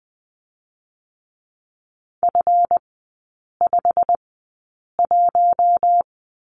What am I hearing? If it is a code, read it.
Morse code, heard as a keyed tone, about 20 wpm, spaced as for 8 wpm, 700 Hz. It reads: F51
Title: 152 Lesson 25e - Letter p - Three Characters 20wpm